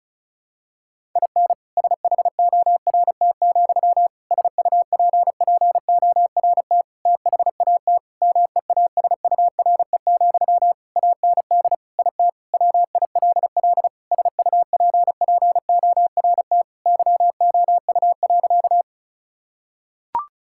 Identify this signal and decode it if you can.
Morse code, heard as a keyed tone, about 36 wpm, 700 Hz. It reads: IN SHORT, SUPPORT THAT MEASURE, AND IT WILL SUPPORT YOU.
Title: Sentences from Top 1000 Words - Review Part 5 - 35wpm